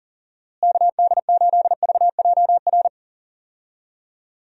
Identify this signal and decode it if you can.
Morse code, heard as a keyed tone, about 37 wpm, 700 Hz. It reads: KD8VJR
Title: USA Call Signs - 6 Characters - Repeated 2x - 40wpm